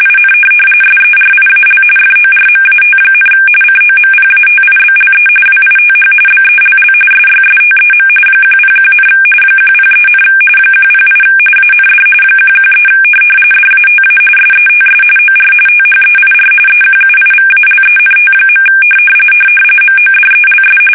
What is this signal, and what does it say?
RTTY, about 45.45 baud. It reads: IN SERIAL FORMAT, COMPATIBLE WITH BELL SYSTEM 101, 103, 108 AND 113 DATA SETS OR THEIR EQUIVALENT. MODELS
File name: fsk850 / teletype-corp